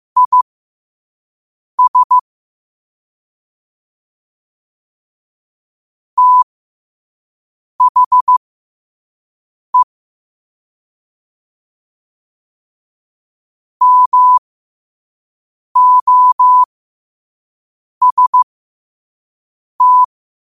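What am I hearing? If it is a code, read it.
Morse code, heard as a keyed tone, about 15 wpm, spaced as for 5 wpm, 1000 Hz. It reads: IS THE MOST
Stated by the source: Morse code sent at 5 words per minute.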